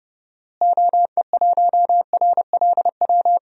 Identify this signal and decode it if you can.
Morse code, heard as a keyed tone, about 32 wpm, 700 Hz. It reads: OE1RLW
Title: DXCC-100 Call Signs - 6 Characters - Repeated 3x - 30wpm